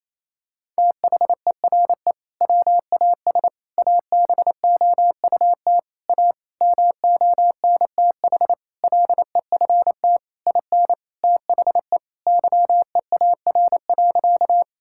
Morse code, heard as a keyed tone, about 28 wpm, 700 Hz. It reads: THERE WAS ABOUT A MONTH LEFT IN THE YEAR.
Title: Sentences from Top 1000 Words - Review Part 5 - 28wpm